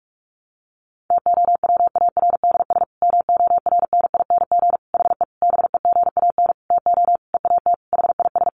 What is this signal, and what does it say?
Morse code, heard as a keyed tone, about 44 wpm, 700 Hz. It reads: TOWARDS MORNING HE BEGAN TO EAT HIS
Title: Sentences from Top 1100 Words 45wpm